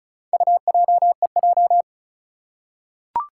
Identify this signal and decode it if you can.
Morse code, heard as a keyed tone, about 36 wpm, 700 Hz. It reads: UJEJ
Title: Four Challenging Letter-Number Combinations 35wpm